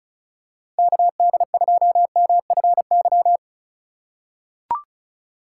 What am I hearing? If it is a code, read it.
Morse code, heard as a keyed tone, about 36 wpm, 700 Hz. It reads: KD2MFY
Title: USA Call Signs - 6 Characters - 2x Character Spacing - 35wpm